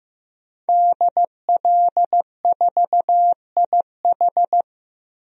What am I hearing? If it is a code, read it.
Morse code, heard as a keyed tone, about 15 wpm, 700 Hz. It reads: DL4IH
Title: DXCC-100 Call Signs - 5 Characters - Repeated 3x - 15wpm